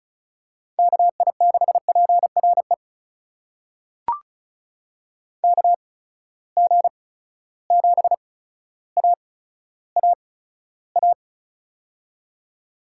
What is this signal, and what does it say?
Morse code, heard as a keyed tone, about 34 wpm, spaced as for 15 wpm, 700 Hz. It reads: KI6PRE KG7AAA
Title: USA Call Signs - 6 Characters - 8x Character Spacing - 35wpm